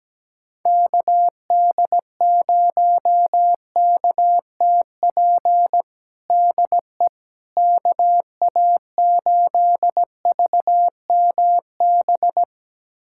Morse code, heard as a keyed tone, about 17 wpm, 700 Hz. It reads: KD0KTP DE KA8VMB
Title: USA Callsign DE Callsign 17wpm